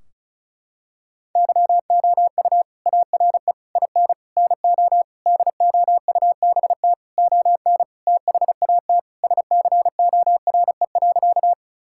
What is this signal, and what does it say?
Morse code, heard as a keyed tone, about 36 wpm, 700 Hz. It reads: YOU ARE IN NO DOUBT ON THAT SCORE.
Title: Sentences from Top 1600 Words 35wpm